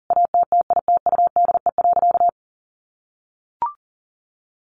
Morse code, heard as a keyed tone, about 40 wpm, 700 Hz. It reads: ATTITUDE.
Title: Sentences from Top 1300 Words 40wpm